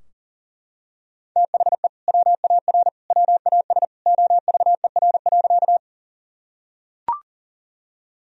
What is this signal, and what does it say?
Morse code, heard as a keyed tone, about 40 wpm, 700 Hz. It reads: THE WAR WAS OVER.
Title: Sentences from Top 1000 Words - Review Part 2 - 40wpm